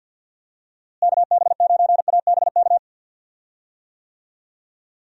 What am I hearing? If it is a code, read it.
Morse code, heard as a keyed tone, about 46 wpm, 700 Hz. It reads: KB9ABK
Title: USA Call Signs - 6 Characters - Repeated 3x - 50wpm